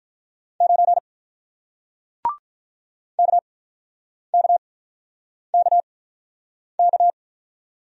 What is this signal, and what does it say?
Morse code, heard as a keyed tone, about 48 wpm, spaced as for 9 wpm, 700 Hz. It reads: 9 KKKK